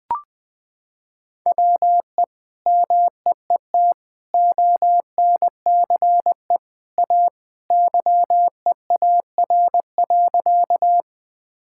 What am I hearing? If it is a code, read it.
Morse code, heard as a keyed tone, about 20 wpm, 700 Hz. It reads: WE MEET ONCE A YEAR.
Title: Sentences with 2-Word Phrases - 20wpm